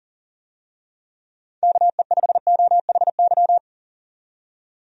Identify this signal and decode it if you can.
Morse code, heard as a keyed tone, about 37 wpm, 700 Hz. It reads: KE5OHY
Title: USA Call Signs - 6 Characters - 40wpm